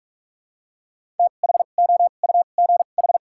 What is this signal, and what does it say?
Morse code, heard as a keyed tone, about 47 wpm, spaced as for 33 wpm, 700 Hz. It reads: THOUGH
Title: 2x Character Spacing - Top 500 Words - 45wpm